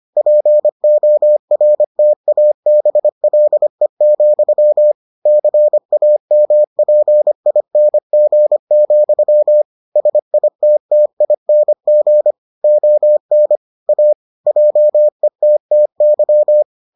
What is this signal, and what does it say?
Morse code, heard as a keyed tone, about 25 wpm, 600 Hz. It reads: PORTABLE, CAMPING, SITTING ON A JETTY